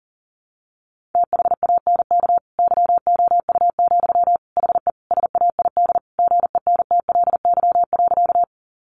Morse code, heard as a keyed tone, about 40 wpm, 700 Hz. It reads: THANK YOU, HE SAID GENTLY.
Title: Sentences from Top 1500 Words 40wpm